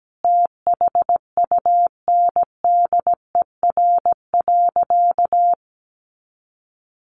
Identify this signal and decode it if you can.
Morse code, heard as a keyed tone, about 17 wpm, 700 Hz. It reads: THUNDER.
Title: A Sound Of Thunder 17wpm